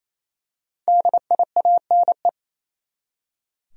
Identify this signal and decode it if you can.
Morse code, heard as a keyed tone, about 28 wpm, 700 Hz. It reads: DIANE
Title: USA Common Names - Max 5 Letters - 28wpm